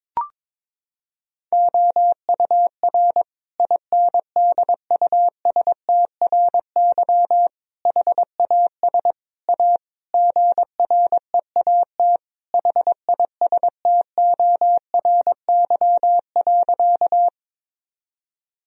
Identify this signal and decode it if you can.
Morse code, heard as a keyed tone, about 22 wpm, 700 Hz. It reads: OUR INDUSTRY HAS A GREAT HISTORY.
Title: Sentences from Top 1000 Words - Review Part 4 - 22wpm